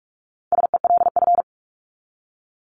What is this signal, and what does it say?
Morse code, heard as a keyed tone, about 44 wpm, 700 Hz. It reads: SELF
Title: Top 500 Words 45wpm